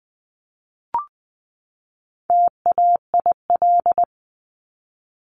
Morse code, heard as a keyed tone, about 20 wpm, 700 Hz. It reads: TAIL
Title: 260 Lesson 42e - Words 20-20wpm